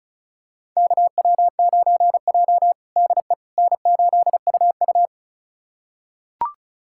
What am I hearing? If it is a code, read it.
Morse code, heard as a keyed tone, about 34 wpm, 700 Hz. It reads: KW9J DE N8UU
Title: USA Callsign DE Callsign - 4 Characters - 35wpm